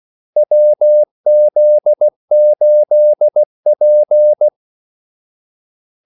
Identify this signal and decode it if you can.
Morse code, heard as a keyed tone, about 16 wpm, 600 Hz. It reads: WZ8P
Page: Kaldesignaler 16 wpm